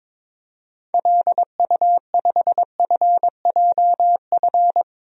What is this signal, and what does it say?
Morse code, heard as a keyed tone, about 22 wpm, 700 Hz. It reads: LU5FJF
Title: DXCC-100 Call Signs - 6 Characters - 22wpm